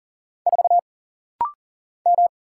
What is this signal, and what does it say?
Morse code, heard as a keyed tone, about 39 wpm, 700 Hz. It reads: 4 M
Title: Single Letter-Number - Rapid-Fire 40wpm